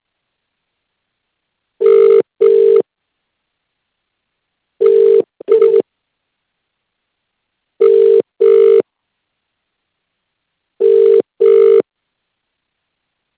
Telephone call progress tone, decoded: ringback tone